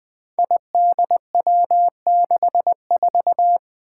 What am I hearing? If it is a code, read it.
Morse code, heard as a keyed tone, about 21 wpm, 700 Hz. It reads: IDW64
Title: Five Challenging Letter-Number Combinations 20wpm